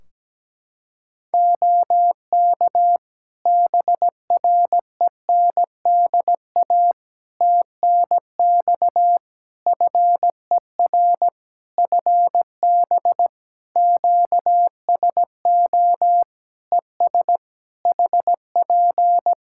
Morse code, heard as a keyed tone, about 17 wpm, 700 Hz. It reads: OK BRENDA TNX FER FB QSO ES HP
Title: Practice QSO - Exchange 4 - Ending - 17wpm